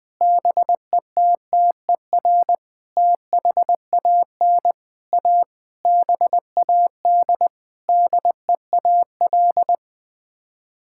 Morse code, heard as a keyed tone, about 21 wpm, 700 Hz. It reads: BETTER THAN A BAD DEAL
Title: Sets of 8 Words - 20wpm